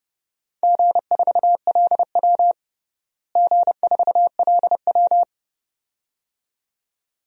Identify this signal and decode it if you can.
Morse code, heard as a keyed tone, about 32 wpm, 700 Hz. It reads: G4LW G4LW
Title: DXCC-100 Call Signs - 4 Characters - Repeated 2x - 30wpm